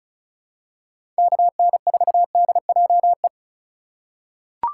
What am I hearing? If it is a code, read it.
Morse code, heard as a keyed tone, about 36 wpm, 700 Hz. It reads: KN4DJE
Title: USA Call Signs - 2x Character Spacing - 35wpm